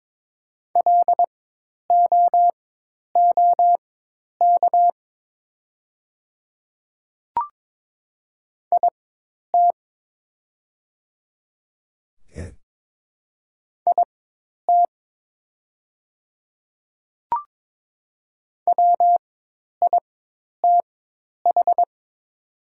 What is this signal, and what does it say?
Morse code, heard as a keyed tone, about 22 wpm, spaced as for 10 wpm, 700 Hz. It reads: LOOK IT IT WITH
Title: 4x Character Spacing - Top 200 Words - 22wpm